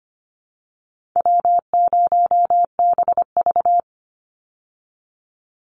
Morse code, heard as a keyed tone, about 25 wpm, 700 Hz. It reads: W0BV
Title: Easy Call Signs Repeated 3x - 25wpm